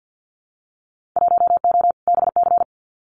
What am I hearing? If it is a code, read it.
Morse code, heard as a keyed tone, about 50 wpm, 700 Hz. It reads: 1O BC
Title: Winter Field Day Exchanges Encore 50wpm